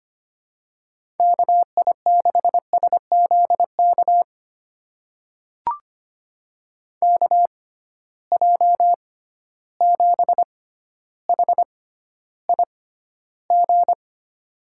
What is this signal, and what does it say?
Morse code, heard as a keyed tone, about 25 wpm, spaced as for 13 wpm, 700 Hz. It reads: KI6SZK KJ7HIG
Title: USA Call Signs - 6 Characters - 6x Character Spacing - 25wpm